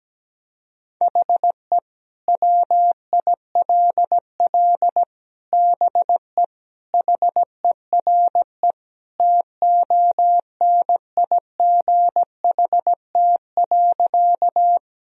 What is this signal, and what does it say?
Morse code, heard as a keyed tone, about 17 wpm, 700 Hz. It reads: HE WILL BE HERE TONIGHT.
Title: Sentences from Top 1000 Words - Review Part 3 - 17wpm